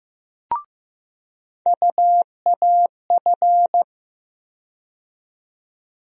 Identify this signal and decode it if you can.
Morse code, heard as a keyed tone, about 15 wpm, 700 Hz. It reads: UAF
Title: USA Call Sign Postfixes - 3 Letters 15wpm